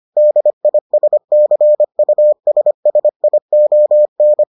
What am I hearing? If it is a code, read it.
Morse code, heard as a keyed tone, about 25 wpm, 600 Hz. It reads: DISCUSSION